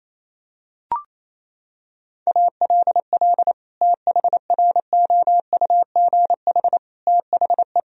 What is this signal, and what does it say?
Morse code, heard as a keyed tone, about 28 wpm, 700 Hz. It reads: ALL THROUGH THE
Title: Sentences from Top 1000 Words - Review Part 2 - 28wpm